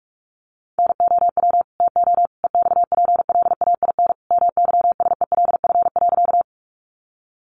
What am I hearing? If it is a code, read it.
Morse code, heard as a keyed tone, about 44 wpm, 700 Hz. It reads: NOW TO EXPLAIN MYSELF.
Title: Sentences from Top 1300 Words 45wpm